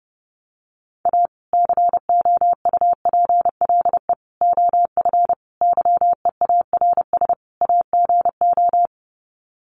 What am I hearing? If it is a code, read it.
Morse code, heard as a keyed tone, about 30 wpm, 700 Hz. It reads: A COUPLE OF YEARS AGO
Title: Sets of 5 Words 30wpm